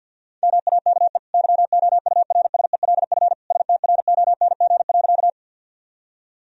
Morse code, heard as a keyed tone, about 50 wpm, 700 Hz. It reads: MAKE YOURSELF STRONG.
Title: Sentences from Top 1000 Words - Review Part 3 - 50wpm